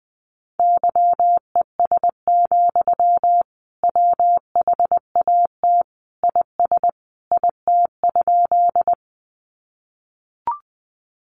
Morse code, heard as a keyed tone, about 20 wpm, 700 Hz. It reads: YES, WHAT IS IT?
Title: Sentences from Top 500 Words 20wpm